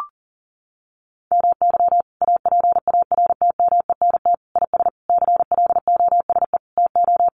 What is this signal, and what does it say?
Morse code, heard as a keyed tone, about 37 wpm, 700 Hz. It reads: MY APARTMENT IS CLOSE TO
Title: Sentences from Top 900 Words 40wpm